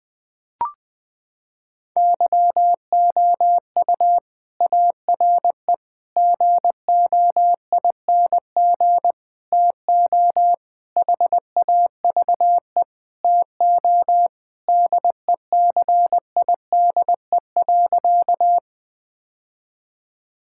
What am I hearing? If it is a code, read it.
Morse code, heard as a keyed tone, about 20 wpm, 700 Hz. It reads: YOU ARE GOING TO HAVE TO DECIDE.
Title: Sentences with 4-Word Phrases - 20wpm